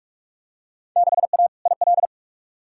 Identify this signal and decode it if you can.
Morse code, heard as a keyed tone, about 41 wpm, 700 Hz. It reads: 6A IL
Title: Field Day Exchanges - 45wpm